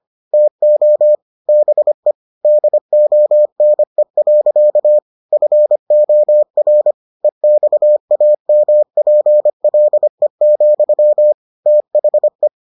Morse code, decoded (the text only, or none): TO BE DONE. FOR EXAMPLE, THE